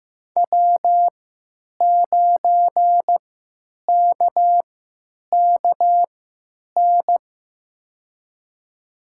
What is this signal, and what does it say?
Morse code, heard as a keyed tone, about 15 wpm, spaced as for 9 wpm, 700 Hz. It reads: W9KKN
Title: USA Call Signs - 5 Characters - 3x Character Spacing - 15wpm